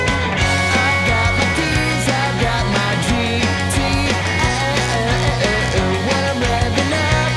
Dial-up modem handshake.